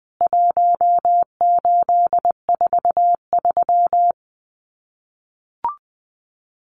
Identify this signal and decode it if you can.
Morse code, heard as a keyed tone, about 20 wpm, 700 Hz. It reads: 1843
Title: CWT Exchange 20wpm